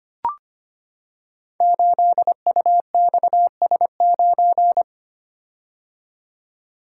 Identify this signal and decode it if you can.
Morse code, heard as a keyed tone, about 25 wpm, 700 Hz. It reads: 8UXS9
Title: Code Groups - 5 Letter:Numbers - 25wpm